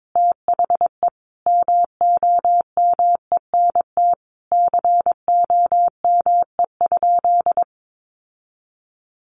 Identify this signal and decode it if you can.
Morse code, heard as a keyed tone, about 22 wpm, 700 Hz. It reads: THE MOMENT COME?
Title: Sentences from Top 1000 Words 22wpm